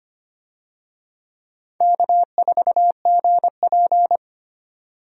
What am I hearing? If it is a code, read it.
Morse code, heard as a keyed tone, about 25 wpm, 700 Hz. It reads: K4GP
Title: USA Call Signs - 4 Characters - 25wpm